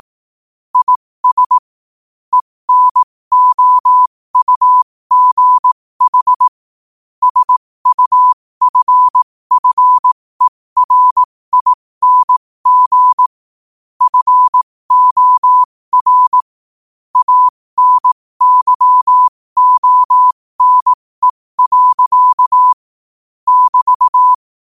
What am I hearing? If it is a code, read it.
Morse code, heard as a keyed tone, about 18 wpm, spaced as for 15 wpm, 1000 Hz. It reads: IS ENOUGH SUFFERING FOR ANYONE. <BT>
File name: qotdfiles / 15wpm